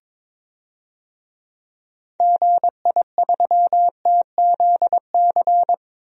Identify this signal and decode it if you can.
Morse code, heard as a keyed tone, about 22 wpm, 700 Hz. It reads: GI3TZC
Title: DXCC-100 Call Signs - 6 Characters - 22wpm